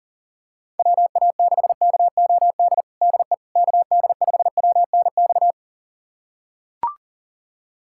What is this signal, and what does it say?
Morse code, heard as a keyed tone, about 40 wpm, 700 Hz. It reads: WA6KOD DE KD5WNX
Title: USA Callsign DE Callsign - 6 Characters - 40wpm